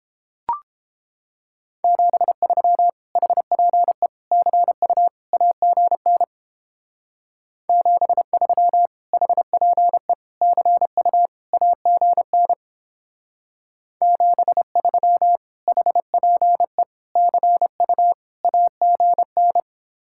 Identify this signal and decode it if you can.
Morse code, heard as a keyed tone, about 29 wpm, 700 Hz. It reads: 73 HPE CU AGN 73 HPE CU AGN 73 HPE CU AGN